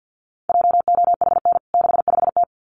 Morse code, heard as a keyed tone, about 50 wpm, 700 Hz. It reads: JOHN 65T
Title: CWT Exchange 50wpm